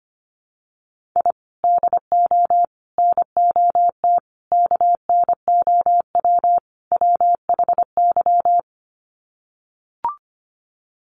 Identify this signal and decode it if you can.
Morse code, heard as a keyed tone, about 25 wpm, 700 Hz. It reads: I DO NOT KNOW WHY
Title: Sets of 5 Words 25wpm